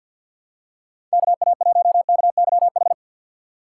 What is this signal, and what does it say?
Morse code, heard as a keyed tone, about 51 wpm, 700 Hz. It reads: KA1KYH
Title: USA Call Signs - 3x Character Spacing - 50wpm